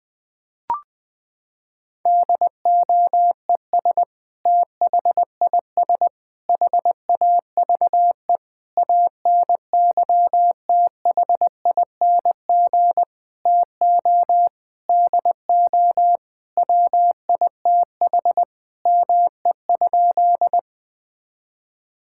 Morse code, heard as a keyed tone, about 20 wpm, 700 Hz. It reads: DOES THIS HAVE ANYTHING TO DO WITH ME?
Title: Sentences with 5-Word Phrases - 20wpm